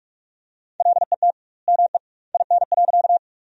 Morse code, heard as a keyed tone, about 41 wpm, 700 Hz. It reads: LET ME IN.
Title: 2x Word-Spacing - Sentences from Top 300 Words - 45wpm